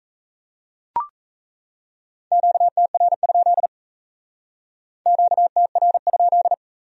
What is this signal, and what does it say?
Morse code, heard as a keyed tone, about 34 wpm, 700 Hz. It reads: QTR? QTR?
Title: Speed-Racing Q-Codes - 28wpm